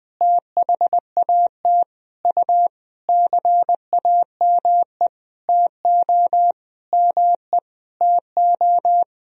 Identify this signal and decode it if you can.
Morse code, heard as a keyed tone, about 20 wpm, 700 Hz. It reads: THAT U CAME TO ME TO
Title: Sentences from Top 200 Words and Abbreviations 20wpm